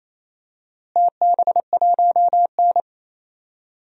Morse code, heard as a keyed tone, about 28 wpm, 700 Hz. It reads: TB1N
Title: DXCC-100 Call Signs - 4 Characters - Repeated 3x - 28wpm